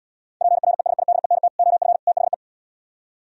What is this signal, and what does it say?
Morse code, heard as a keyed tone, about 77 wpm, 700 Hz. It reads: PRESIDENT OF THE